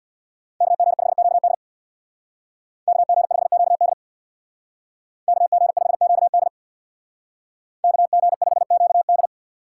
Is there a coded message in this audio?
Yes: Morse code from 0.6 to 9.3 s, heard as a keyed tone, about 64 wpm, 700 Hz. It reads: KG5QD KG5QD KG5QD KG5QD